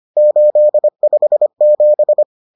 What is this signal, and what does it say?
Morse code, heard as a keyed tone, about 25 wpm, 600 Hz. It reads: 857